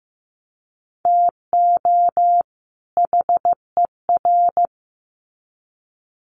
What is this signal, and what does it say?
Morse code, heard as a keyed tone, about 15 wpm, 700 Hz. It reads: TO HER
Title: Sets of 2 Words 15wpm